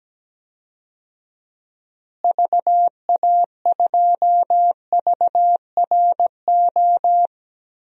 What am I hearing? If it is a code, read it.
Morse code, heard as a keyed tone, about 17 wpm, 700 Hz. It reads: VA2VRO
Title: DXCC-100 Call Signs - 6 Characters - Repeated 2x - 17wpm